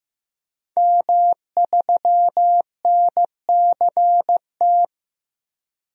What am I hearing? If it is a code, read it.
Morse code, heard as a keyed tone, about 15 wpm, 700 Hz. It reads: M3NCT
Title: DXCC-100 Call Signs - 5 Characters - 15wpm